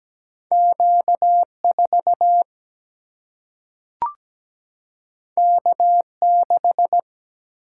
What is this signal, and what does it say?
Morse code, heard as a keyed tone, about 17 wpm, 700 Hz. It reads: Q4 K6
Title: Single-Letter Single-Number 17wpm